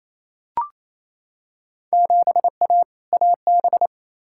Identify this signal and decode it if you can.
Morse code, heard as a keyed tone, about 28 wpm, 700 Hz. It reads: 7A AB
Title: Field Day Exchanges Encore - 28wpm